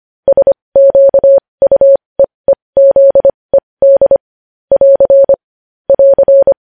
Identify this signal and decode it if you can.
Morse code, heard as a keyed tone, about 25 wpm, spaced as for 20 wpm, 550 Hz. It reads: SQUEEZED <AR> <AR>
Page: News Headlines in Morse Code at 20 WPM